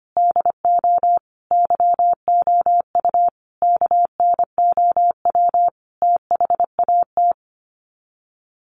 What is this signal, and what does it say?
Morse code, heard as a keyed tone, about 25 wpm, 700 Hz. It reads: DO YOU KNOW THAT
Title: Sets of 5 Words Encore 25wpm